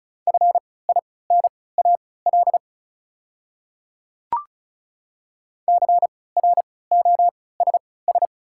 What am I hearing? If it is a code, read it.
Morse code, heard as a keyed tone, about 36 wpm, spaced as for 20 wpm, 690 Hz. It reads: FINAL CROSS
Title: 3x Character Spacing - Top 500 Words - 35wpm